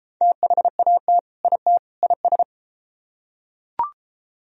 Morse code, heard as a keyed tone, about 30 wpm, 700 Hz. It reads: THAT IT IS